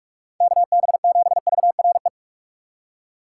Morse code, heard as a keyed tone, about 47 wpm, 700 Hz. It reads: KB7VRE
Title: USA Call Signs - 6 Characters - 8x Character Spacing - 45wpm